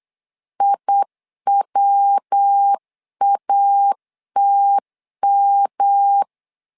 Morse code, heard as a keyed tone, about 8 wpm, 800 Hz. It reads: IWATM
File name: LEZIONI / 3 MWRO / 3 MW 8wpm 2501